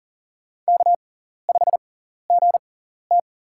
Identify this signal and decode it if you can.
Morse code, heard as a keyed tone, about 40 wpm, spaced as for 14 wpm, 700 Hz. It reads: K5GT
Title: USA Call Signs - 6x Character Spacing - 40wpm